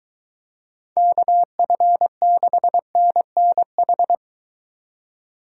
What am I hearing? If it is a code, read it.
Morse code, heard as a keyed tone, about 23 wpm, 700 Hz. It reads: KF6NNH